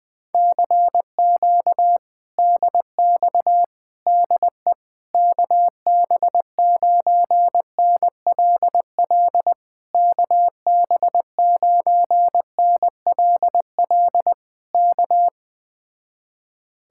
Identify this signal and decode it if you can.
Morse code, heard as a keyed tone, about 20 wpm, 700 Hz. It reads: CQ DX DE KB9NLL KB9NLL K